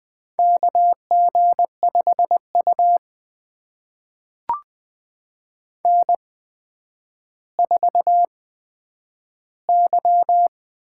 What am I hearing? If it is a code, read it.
Morse code, heard as a keyed tone, about 20 wpm, spaced as for 10 wpm, 700 Hz. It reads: KG5U N4Y
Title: USA Call Signs - 4 Characters - 8x Character Spacing - 20wpm